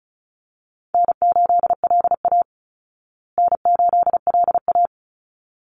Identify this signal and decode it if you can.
Morse code, heard as a keyed tone, about 34 wpm, 700 Hz. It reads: N8LA N8LA
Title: Easy Call Signs Repeated 2x - 35wpm